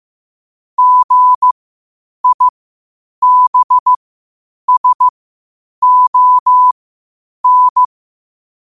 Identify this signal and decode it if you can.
Morse code, heard as a keyed tone, about 15 wpm, spaced as for 8 wpm, 1000 Hz. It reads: GIBSON
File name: qotdfiles / 8wpm